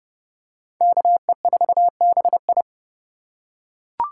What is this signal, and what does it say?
Morse code, heard as a keyed tone, about 27 wpm, 700 Hz. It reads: KE4BI